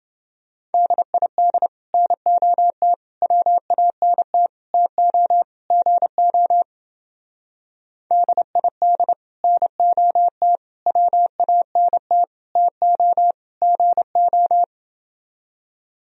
Morse code, heard as a keyed tone, about 28 wpm, 700 Hz. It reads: DID NOT WANT TO GO DID NOT WANT TO GO